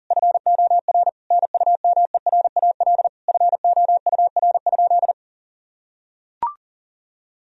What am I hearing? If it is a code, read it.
Morse code, heard as a keyed tone, about 40 wpm, 700 Hz. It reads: FOR NUMERAL FOUR?
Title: Sentences from Top 1000 Words - Review Part 2 - 40wpm